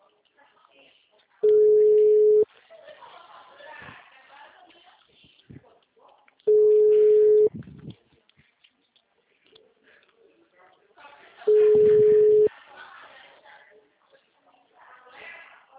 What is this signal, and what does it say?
Ringback tone (Europe, 425 Hz), with speech.
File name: out- -2017- - - .68112